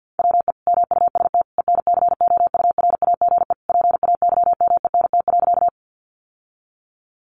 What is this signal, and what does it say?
Morse code, heard as a keyed tone, about 50 wpm, 700 Hz. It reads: WE MUST ENCOURAGE PAYMENT.
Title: Sentences from Top 1400 Words 50wpm